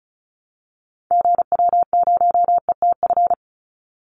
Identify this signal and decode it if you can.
Morse code, heard as a keyed tone, about 34 wpm, 700 Hz. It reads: GW0ETF
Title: Call Signs - 35wpm